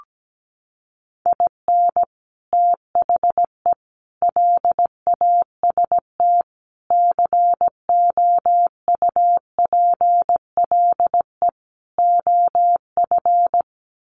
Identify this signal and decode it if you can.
Morse code, heard as a keyed tone, about 17 wpm, 700 Hz. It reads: IN THE LAST COUPLE OF
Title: Sets of 5 Words 17wpm